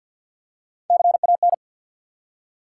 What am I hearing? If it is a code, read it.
Morse code, heard as a keyed tone, about 53 wpm, 700 Hz. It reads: CAN
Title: Top 180 QSO Elements 50wpm